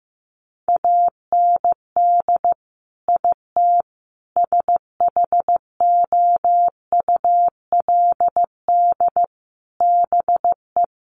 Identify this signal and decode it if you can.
Morse code, heard as a keyed tone, about 15 wpm, 700 Hz. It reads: AND IT SHOULD BE
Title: Sentences from Top 1500 Words 15wpm